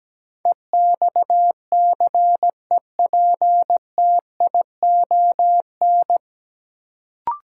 Morse code, heard as a keyed tone, about 17 wpm, 700 Hz. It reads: EXCEPTION
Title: Sets of 4 Words - New Word Drill - 17wpm